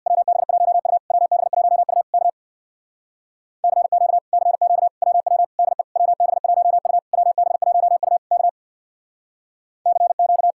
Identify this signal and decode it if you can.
Morse code, heard as a keyed tone, about 62 wpm, 700 Hz. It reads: WB1U WB1U K CQ CQ WW DE WB1U WB1U K CQ
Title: Speed-Racing Common QSO Phrases - 50wpm